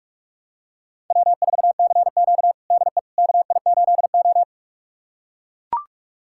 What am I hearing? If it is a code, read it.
Morse code, heard as a keyed tone, about 42 wpm, 700 Hz. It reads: W4CQ DE KI8O
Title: USA Callsign DE Callsign - 4 Characters - 45wpm